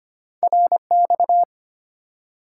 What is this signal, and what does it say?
Morse code, heard as a keyed tone, about 25 wpm, 700 Hz. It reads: RX
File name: Top 90 QSO Elements - 3x Character Spacing - 25wpm